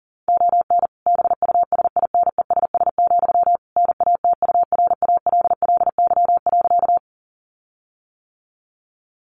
Morse code, heard as a keyed tone, about 40 wpm, 700 Hz. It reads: ON BUSINESS, NATURALLY.
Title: Sentences from Top 1300 Words 40wpm